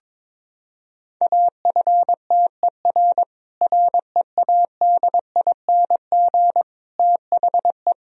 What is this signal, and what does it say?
Morse code, heard as a keyed tone, about 22 wpm, 700 Hz. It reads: AFTER READING THE
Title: Sentences from Top 1000 Words - Review Part 3 - 22wpm